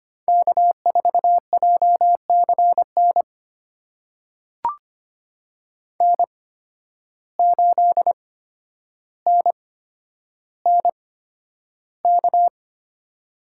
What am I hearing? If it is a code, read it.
Morse code, heard as a keyed tone, about 25 wpm, spaced as for 11 wpm, 700 Hz. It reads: K4JCN N8NNK
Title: USA Call Signs - 8x Character Spacing - 25wpm